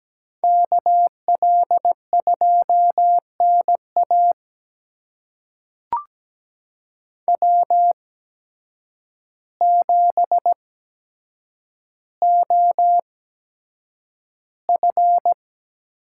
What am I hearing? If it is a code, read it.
Morse code, heard as a keyed tone, about 17 wpm, spaced as for 8 wpm, 700 Hz. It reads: KL2NA W7OF
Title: USA Call Signs - 8x Character Spacing - 17wpm